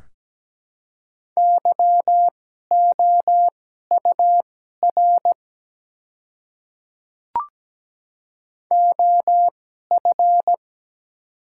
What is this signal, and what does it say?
Morse code, heard as a keyed tone, about 17 wpm, spaced as for 12 wpm, 700 Hz. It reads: YOUR OF
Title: 2x Character Spacing - Top 100 Words - 17wpm